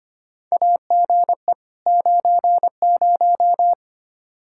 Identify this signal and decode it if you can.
Morse code, heard as a keyed tone, about 25 wpm, 700 Hz. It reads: AGE 90
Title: Practice QSO - Exchange 3 - More Info - 25wpm